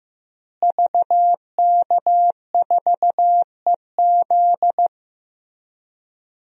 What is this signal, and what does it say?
Morse code, heard as a keyed tone, about 15 wpm, 700 Hz. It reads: VK4EZ
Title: DXCC-100 Call Signs - 5 Characters - Repeated 3x - 15wpm